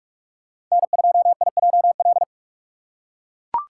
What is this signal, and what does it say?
Morse code, heard as a keyed tone, about 44 wpm, 700 Hz. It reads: N2IJL
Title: USA Call Signs - 5 Characters - 6x Character Spacing - 45wpm